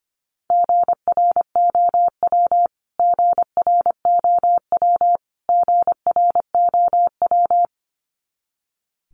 Morse code, heard as a keyed tone, about 25 wpm, 700 Hz. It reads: GROW GROW GROW
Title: Top 300 Words Repeated 25wpm